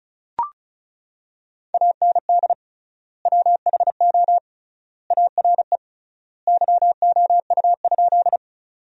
Morse code, heard as a keyed tone, about 36 wpm, 700 Hz. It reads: AND WHO ARE YOU?
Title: 3x Word-Spacing - Sentences from Top 100 Words - 35wpm